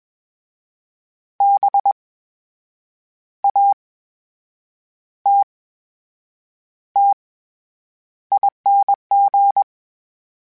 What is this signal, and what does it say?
Morse code, heard as a keyed tone, about 21 wpm, spaced as for 7 wpm, 800 Hz. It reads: BATTING
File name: Special INGs-mp3s / Special INGs-mp3s / VerySlowToFaster Ing @20